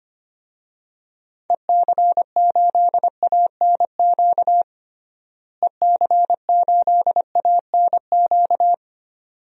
Morse code, heard as a keyed tone, about 25 wpm, 700 Hz. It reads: EC8ANQ EC8ANQ
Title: DXCC-100 Call Signs - 6 Characters - Repeated 2x - 25wpm